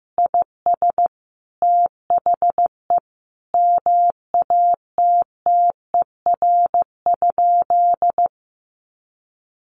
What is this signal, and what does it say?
Morse code, heard as a keyed tone, about 15 wpm, 700 Hz. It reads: IS THE MATTER?
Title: Sentences from Top 1100 Words 15wpm